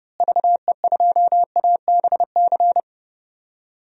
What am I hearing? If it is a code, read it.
Morse code, heard as a keyed tone, about 29 wpm, 700 Hz. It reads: VE2ABC
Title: DXCC-100 Call Signs - 6 Characters - Repeated 3x - 30wpm